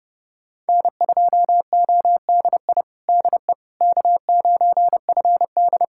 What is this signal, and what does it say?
Morse code, heard as a keyed tone, about 32 wpm, 700 Hz. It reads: N2ODI DE K9FD
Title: Practice QSO - Exchange 4 - Ending - 30wpm